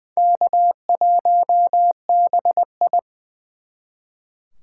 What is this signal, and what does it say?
Morse code, heard as a keyed tone, about 24 wpm, 700 Hz. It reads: K1BI
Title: Speed Racing - USA Call Signs 20wpm